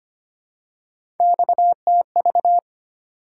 Morse code, heard as a keyed tone, about 25 wpm, 700 Hz. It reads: XTV
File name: Three Letter-Number Combinations - 25wpm